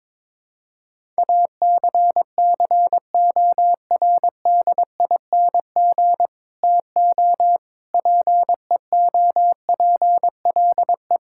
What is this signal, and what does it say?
Morse code, heard as a keyed tone, about 22 wpm, 700 Hz. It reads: ACCORDING TO PEOPLE
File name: practice / 22wpm / Sets of 7 Words - 22wpm